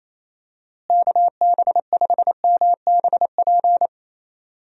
Morse code, heard as a keyed tone, about 28 wpm, 700 Hz. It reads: KB5MBP
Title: USA Call Signs - 6 Characters - Repeated 2x - 28wpm